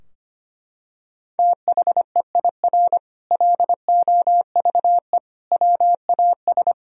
Morse code, heard as a keyed tone, about 25 wpm, 700 Hz. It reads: THEIR LOVE WAS
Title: Sentences from Top 1000 Words - Review Part 4 - 25wpm